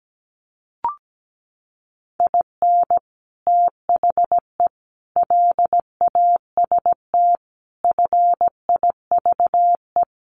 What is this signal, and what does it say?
Morse code, heard as a keyed tone, about 17 wpm, 700 Hz. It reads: IN THE LAST FIVE
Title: Sets of 5 Words Encore 17wpm